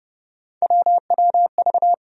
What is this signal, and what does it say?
Morse code, heard as a keyed tone, about 30 wpm, 700 Hz. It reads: WWV
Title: USA Call Sign Postfixes - 3 Letters 30wpm